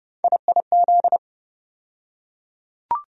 Morse code, heard as a keyed tone, about 32 wpm, 700 Hz. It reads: IIZ
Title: USA Call Sign Postfixes - 3 Letters 30wpm